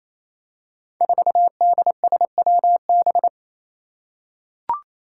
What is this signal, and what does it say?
Morse code, heard as a keyed tone, about 28 wpm, 700 Hz. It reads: 4DSWB
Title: Five Challenging Letter-Number Combinations 28wpm